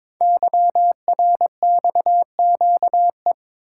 Morse code, heard as a keyed tone, about 22 wpm, 690 Hz. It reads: YRXQE
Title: Code Groups - 5 Letters - 22wpm